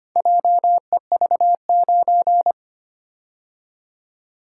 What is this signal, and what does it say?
Morse code, heard as a keyed tone, about 25 wpm, 700 Hz. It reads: JEV9
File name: practice / 25wpm / Four Letter-Number Combinations - 25wpm